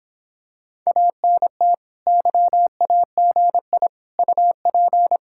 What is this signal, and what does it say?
Morse code, heard as a keyed tone, about 26 wpm, 700 Hz. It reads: ANT YAGI UP